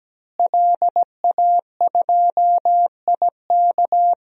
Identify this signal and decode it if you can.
Morse code, heard as a keyed tone, about 17 wpm, 700 Hz. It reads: LA2IK
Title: DXCC-100 Call Signs - 5 Characters - Repeated 2x - 17wpm